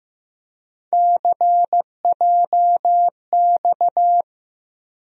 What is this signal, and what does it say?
Morse code, heard as a keyed tone, about 15 wpm, 700 Hz. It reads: CJX
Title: USA Call Sign Postfixes - 3 Letters 15wpm